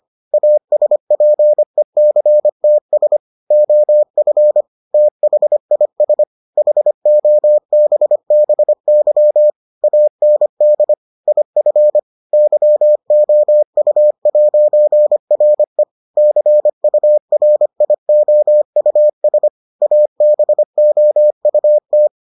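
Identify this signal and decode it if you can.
Morse code, heard as a keyed tone, about 25 wpm, 600 Hz. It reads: ASPECTS OF THIS HOBBY AND IF YOU'RE CURIOUS ABOUT
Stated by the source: Morse edition of the Foundations of Amateur Radio episodes.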